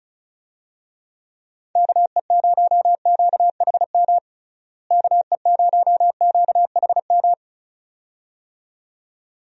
Morse code, heard as a keyed tone, about 36 wpm, 700 Hz. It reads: KE0QHM KE0QHM
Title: USA Call Signs - 6 Characters - Repeated 2x - 35wpm